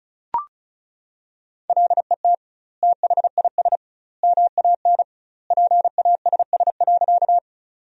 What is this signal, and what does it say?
Morse code, heard as a keyed tone, about 36 wpm, 700 Hz. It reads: LET THIS MAN PASS.
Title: 2x Word-Spacing - Sentences from Top 400 Words - 35wpm